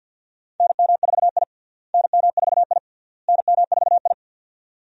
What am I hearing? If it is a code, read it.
Morse code, heard as a keyed tone, about 49 wpm, 700 Hz. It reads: NM4I NM4I NM4I
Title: USA Call Signs - 4 Characters - Repeated 3x - 50wpm